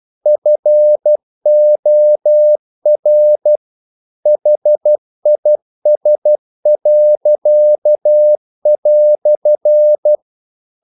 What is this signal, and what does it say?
Morse code, heard as a keyed tone, about 12 wpm, 600 Hz. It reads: FOR HIS."
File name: CW / Practice / EnglishQuotations0312